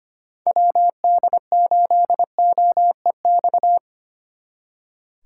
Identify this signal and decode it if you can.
Morse code, heard as a keyed tone, about 30 wpm, 700 Hz. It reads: WD8OEX
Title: Speed Racing - USA Call Signs 25wpm